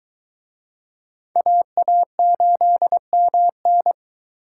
Morse code, heard as a keyed tone, about 24 wpm, 700 Hz. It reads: AA8MN